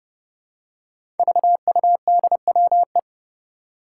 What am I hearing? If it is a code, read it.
Morse code, heard as a keyed tone, about 30 wpm, 700 Hz. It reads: VUDWE
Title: Five Challenging Letter-Number Combinations 30wpm